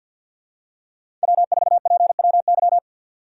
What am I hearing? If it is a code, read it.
Morse code, heard as a keyed tone, about 53 wpm, 700 Hz. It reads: W4PWY
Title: USA Call Signs - 5 Characters - 6x Character Spacing - 50wpm